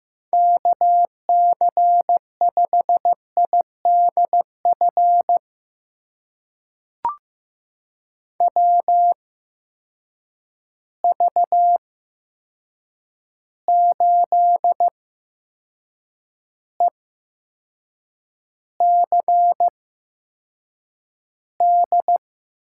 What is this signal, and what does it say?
Morse code, heard as a keyed tone, about 15 wpm, spaced as for 6 wpm, 700 Hz. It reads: KC5IDF WV8ECD
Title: USA Call Signs - 6 Characters - 8x Character Spacing - 15wpm